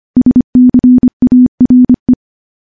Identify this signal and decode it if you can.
Morse code, heard as a keyed tone, about 25 wpm, 250 Hz. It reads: SCARE
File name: morse / morse-training-25WPM